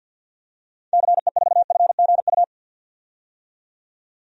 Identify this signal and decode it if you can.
Morse code, heard as a keyed tone, about 53 wpm, 700 Hz. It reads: CE4FGU
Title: DXCC-100 Call Signs - 6 Characters - 50wpm